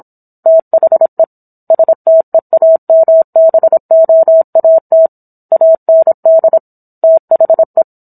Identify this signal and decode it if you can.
Morse code, heard as a keyed tone, about 27 wpm, 650 Hz. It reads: THE STEAMBOAT AND THE